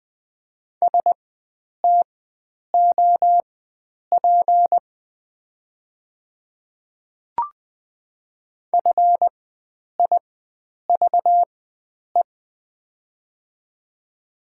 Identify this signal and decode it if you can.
Morse code, heard as a keyed tone, about 20 wpm, spaced as for 9 wpm, 700 Hz. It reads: STOP FIVE
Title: 4x Character Spacing - Top 500 Words - 20wpm